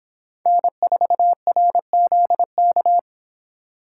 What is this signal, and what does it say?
Morse code, heard as a keyed tone, about 26 wpm, 700 Hz. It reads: N4RZK